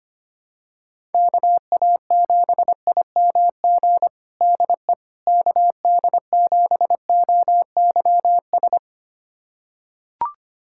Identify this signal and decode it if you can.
Morse code, heard as a keyed tone, about 25 wpm, 700 Hz. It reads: KA7IMG DE KD7OYS
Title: USA Callsign DE Callsign - 6 Characters - 25wpm